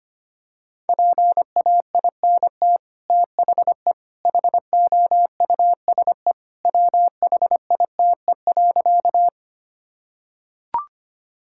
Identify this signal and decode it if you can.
Morse code, heard as a keyed tone, about 25 wpm, 700 Hz. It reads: PAINT THE HOUSE WHITE.
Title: Sentences from Top 1000 Words - Review Part 6 - 25wpm